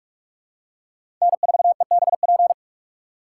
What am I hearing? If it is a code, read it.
Morse code, heard as a keyed tone, about 47 wpm, 690 Hz. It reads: N4EBP
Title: USA Call Signs - 5 Characters - 6x Character Spacing - 45wpm